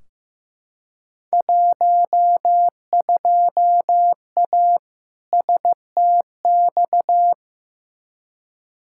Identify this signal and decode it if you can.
Morse code, heard as a keyed tone, about 15 wpm, 700 Hz. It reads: 12A STX
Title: Field Day Exchanges Encore - 15wpm